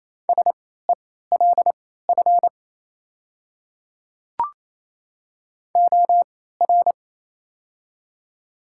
Morse code, heard as a keyed tone, about 28 wpm, spaced as for 16 wpm, 700 Hz. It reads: SELF OR
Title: 3x Character Spacing - Top 300 Words - 28wpm